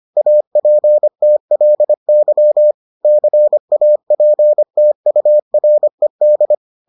Morse code, heard as a keyed tone, about 25 wpm, 600 Hz. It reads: APTLY CAPTURED